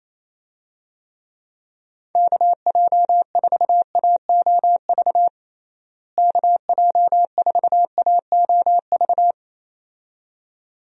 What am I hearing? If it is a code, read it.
Morse code, heard as a keyed tone, about 28 wpm, 700 Hz. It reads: KJ4AOV KJ4AOV
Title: USA Call Signs - 6 Characters - Repeated 2x - 28wpm